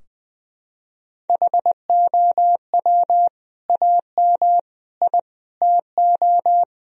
Morse code, heard as a keyed tone, about 20 wpm, 700 Hz. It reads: HOW AM I TO
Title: Sentences from Top 1000 Words - Review Part 4 - 20wpm